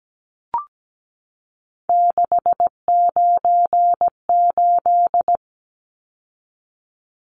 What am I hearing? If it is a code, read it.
Morse code, heard as a keyed tone, about 17 wpm, 700 Hz. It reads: 698
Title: Numbers - 3-Digit 17wpm